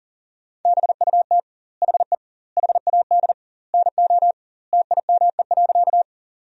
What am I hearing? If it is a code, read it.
Morse code, heard as a keyed tone, about 37 wpm, 700 Hz. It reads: BUT HE HAD NO TIME.
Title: 2x Word-Spacing - Sentences from Top 100 Words - 40wpm